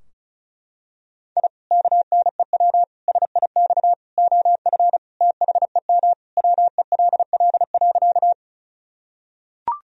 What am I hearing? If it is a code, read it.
Morse code, heard as a keyed tone, about 34 wpm, 700 Hz. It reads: I KNEW SIX OF THEM WELL.
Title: Sentences from Top 1000 Words - Review Part 2 - 35wpm